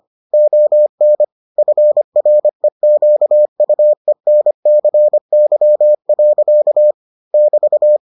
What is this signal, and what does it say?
Morse code, heard as a keyed tone, about 25 wpm, 600 Hz. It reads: ON FREQUENCY. <BT>